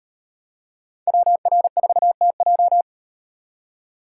Morse code, heard as a keyed tone, about 36 wpm, 700 Hz. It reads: WR4TJ